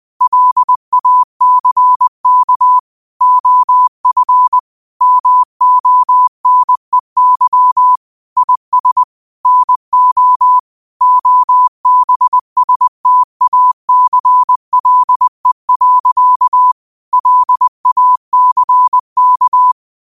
Morse code, heard as a keyed tone, about 20 wpm, 1000 Hz. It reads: LACK OF MONEY IS NO OBSTACLE. LACK